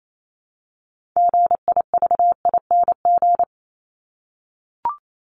Morse code, heard as a keyed tone, about 28 wpm, 700 Hz. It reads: GIVING
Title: ING Suffix - Top 100 Words with Base Words 28wpm